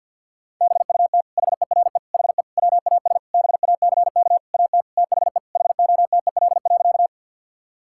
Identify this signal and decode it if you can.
Morse code, heard as a keyed tone, about 50 wpm, 700 Hz. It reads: BUT HERE HE WAS BACK AT THE HOTEL.
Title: Sentences from Top 1600 Words 50wpm